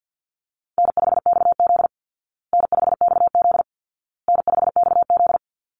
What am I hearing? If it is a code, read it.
Morse code, heard as a keyed tone, about 48 wpm, 700 Hz. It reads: N5XZ N5XZ N5XZ
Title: Easy Call Signs Repeated 3x - 50wpm